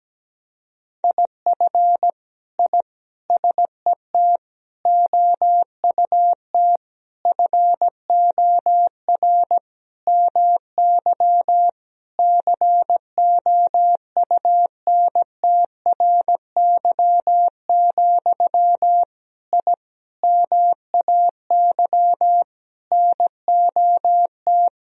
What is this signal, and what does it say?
Morse code, heard as a keyed tone, about 17 wpm, 700 Hz. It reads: IF I SET OUT FOR MY COUNTRY, I MAY NOT
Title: Sentences from Top 1000 Words - Review Part 2 - 17wpm